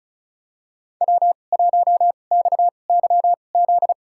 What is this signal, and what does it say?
Morse code, heard as a keyed tone, about 36 wpm, spaced as for 25 wpm, 700 Hz. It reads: W1XYZ
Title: USA Call Signs - 5 Characters - 2x Character Spacing - 35wpm